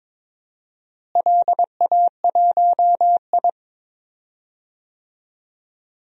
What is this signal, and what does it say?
Morse code, heard as a keyed tone, about 22 wpm, 700 Hz. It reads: LA1I
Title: DXCC-100 Call Signs - 4 Characters - Repeated 2x - 22wpm